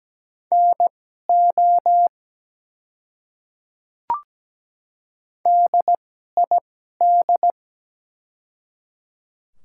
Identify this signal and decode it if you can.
Morse code, heard as a keyed tone, about 17 wpm, spaced as for 12 wpm, 700 Hz. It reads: NO DID
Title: 2x Character Spacing - Top 100 Words - 17wpm